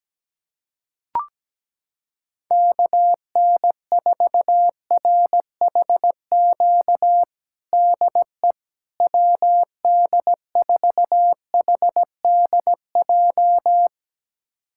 Morse code, heard as a keyed tone, about 17 wpm, 700 Hz. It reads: KN4RHQ DE WD4HDJ
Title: USA Callsign DE Callsign - 6 Characters - 17wpm